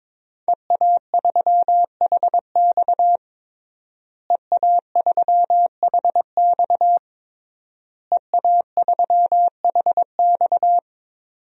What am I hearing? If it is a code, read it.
Morse code, heard as a keyed tone, about 21 wpm, 700 Hz. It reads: EA3HX EA3HX EA3HX
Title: DXCC-100 Call Signs - 5 Characters - Repeated 3x - 22wpm